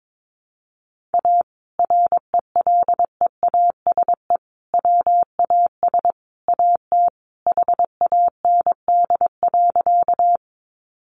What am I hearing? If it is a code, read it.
Morse code, heard as a keyed tone, about 22 wpm, 700 Hz. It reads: A RELEASE WAS AT HAND.
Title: Sentences from Top 1100 Words 22wpm